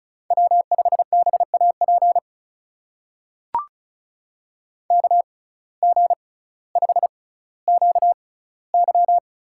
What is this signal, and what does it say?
Morse code, heard as a keyed tone, about 36 wpm, spaced as for 18 wpm, 700 Hz. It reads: W5BAP KG5QY
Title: USA Call Signs - 6x Character Spacing - 35wpm